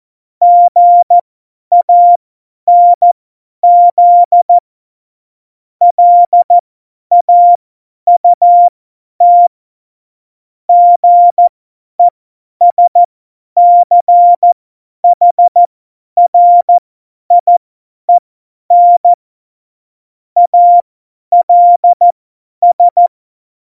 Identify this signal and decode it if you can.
Morse code, heard as a keyed tone, about 14 wpm, spaced as for 10 wpm, 700 Hz. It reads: GANZ LAUT GESCHRIEN ALS
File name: Morsekurs / Witze / Witz 26. Dezember 2017, 50 F70